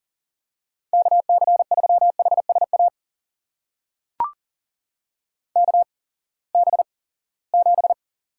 Morse code, heard as a keyed tone, about 40 wpm, spaced as for 23 wpm, 700 Hz. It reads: KC3HSA KB7
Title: USA Call Signs - 6 Characters - 8x Character Spacing - 40wpm